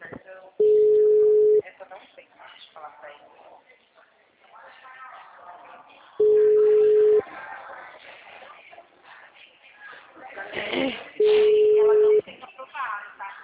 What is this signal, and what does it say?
Ringback tone (Europe, 425 Hz), with speech.
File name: out- -2002- - - .30232